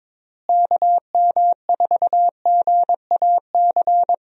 Morse code, heard as a keyed tone, about 22 wpm, 700 Hz. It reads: KM4GAC
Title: USA Call Signs - 6 Characters - Repeated 3x - 22wpm